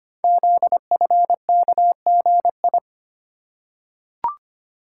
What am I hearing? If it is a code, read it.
Morse code, heard as a keyed tone, about 25 wpm, 700 Hz. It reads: ZFKGI
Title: Code Groups - 5 Letters - 25wpm